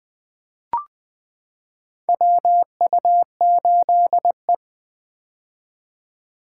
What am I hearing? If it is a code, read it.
Morse code, heard as a keyed tone, about 20 wpm, 700 Hz. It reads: WU8E
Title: 2023 POTA Hunter Call Signs - 20wpm